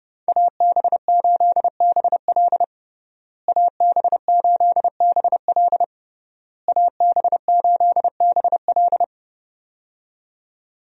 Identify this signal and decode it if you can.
Morse code, heard as a keyed tone, about 30 wpm, 700 Hz. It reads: AB8BL AB8BL AB8BL
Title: USA Call Signs - 5 Characters - Repeated 3x - 30wpm